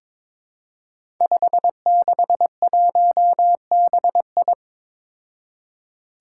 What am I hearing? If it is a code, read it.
Morse code, heard as a keyed tone, about 22 wpm, 700 Hz. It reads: 561BI
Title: Five Challenging Letter-Number Combinations 22wpm